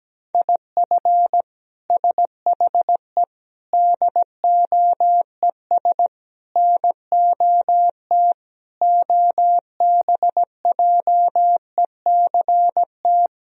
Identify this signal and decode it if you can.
Morse code, heard as a keyed tone, about 17 wpm, 700 Hz. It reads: IF SHE DOES NOT OBJECT
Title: Sentences from Top 1000 Words - Review Part 3 - 17wpm